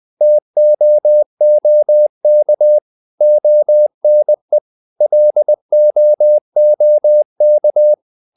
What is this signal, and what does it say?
Morse code, heard as a keyed tone, about 20 wpm, 600 Hz. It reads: TOOK ONE LOOK
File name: the old man and the sea part 2 20wpm